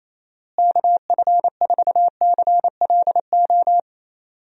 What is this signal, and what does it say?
Morse code, heard as a keyed tone, about 28 wpm, 700 Hz. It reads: KF4CLO
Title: USA Call Signs - 6 Characters - Repeated 3x - 28wpm